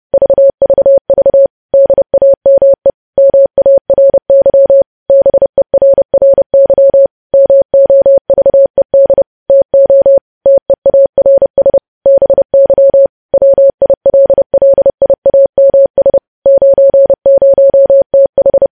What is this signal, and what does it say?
Morse code, heard as a keyed tone, about 30 wpm, 550 Hz. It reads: VVV DAME MARY BERRY MOVED TO TEARS BY WILLIAMS 90TH
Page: news headlines in morse code at 30 wpm